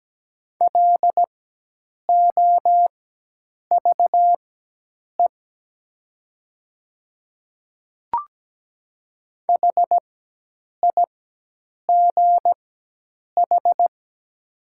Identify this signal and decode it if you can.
Morse code, heard as a keyed tone, about 17 wpm, spaced as for 8 wpm, 700 Hz. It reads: LOVE HIGH
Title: 4x Character Spacing - Top 500 Words - 17wpm